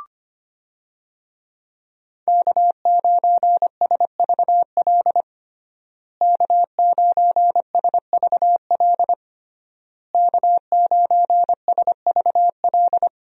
Morse code, heard as a keyed tone, about 25 wpm, 700 Hz. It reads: K9SVL K9SVL K9SVL
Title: USA Call Signs - 5 Characters - Repeated 3x - 25wpm